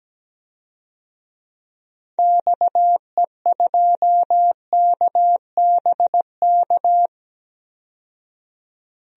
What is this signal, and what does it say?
Morse code, heard as a keyed tone, about 17 wpm, 700 Hz. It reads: XE2KBK
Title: DXCC-100 Call Signs - 6 Characters - 17wpm